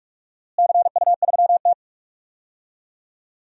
Morse code, heard as a keyed tone, about 44 wpm, 700 Hz. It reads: CU3T
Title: DXCC-100 Call Signs - 4 Characters - Repeated 3x - 45wpm